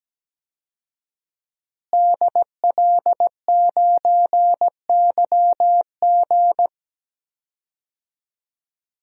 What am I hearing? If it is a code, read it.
Morse code, heard as a keyed tone, about 17 wpm, 700 Hz. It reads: DL9YG
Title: DXCC-100 Call Signs - 5 Characters - Repeated 2x - 17wpm